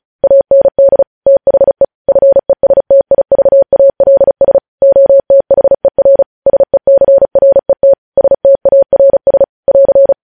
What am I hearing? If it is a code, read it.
Morse code, heard as a keyed tone, about 36 wpm, 550 Hz. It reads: AND THE FESTIVALS OTHER SECRET STARS <AR>
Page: News Headlines in Morse Code at 35 WPM